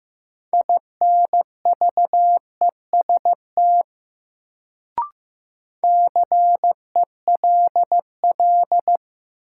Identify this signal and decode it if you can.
Morse code, heard as a keyed tone, about 15 wpm, 700 Hz. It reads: INVEST CELL
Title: Sentences from Top 1100 Words - New Word Drill Alt - 15wpm